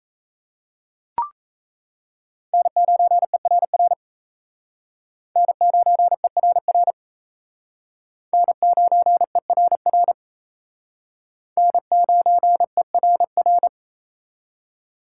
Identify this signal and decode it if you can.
Morse code, heard as a keyed tone, about 35 wpm, 700 Hz. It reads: N9ERR N9ERR N9ERR N9ERR